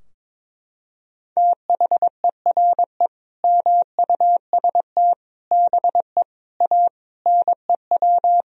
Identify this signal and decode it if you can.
Morse code, heard as a keyed tone, about 22 wpm, 700 Hz. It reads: THERE MUST BE A NEW
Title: Sentences from Top 1000 Words - Review Part 3 - 22wpm